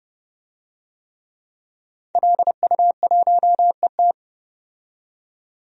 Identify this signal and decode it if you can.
Morse code, heard as a keyed tone, about 30 wpm, 700 Hz. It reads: LU1ET
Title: DXCC-100 Call Signs - 5 Characters - 30wpm